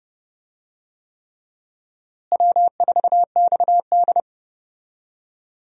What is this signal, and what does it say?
Morse code, heard as a keyed tone, about 30 wpm, 700 Hz. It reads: W4XD
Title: USA Call Signs - 4 Characters - Repeated 3x - 30wpm